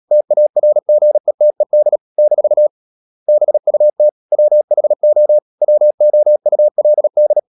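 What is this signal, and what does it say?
Morse code, heard as a keyed tone, about 37 wpm, 600 Hz. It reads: TARGETED - BUT WHO WOULD
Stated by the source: Morse code practice at 35–39 wpm sending English text of Ken Follett's novel Never, from CW med Gnister.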